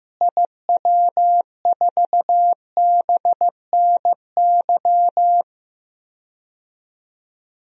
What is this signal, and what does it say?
Morse code, heard as a keyed tone, about 15 wpm, 700 Hz. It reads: IW4BNY
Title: DXCC-100 Call Signs - 6 Characters - Repeated 3x - 15wpm